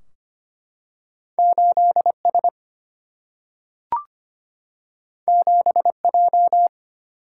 Morse code, heard as a keyed tone, about 25 wpm, 700 Hz. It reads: 8 S 7 J
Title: Single-Letter Single-Number 25wpm